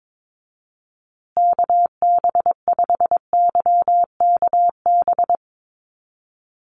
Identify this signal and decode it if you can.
Morse code, heard as a keyed tone, about 22 wpm, 700 Hz. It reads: KB5YKB
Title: USA Call Signs Repeated 2x 22wpm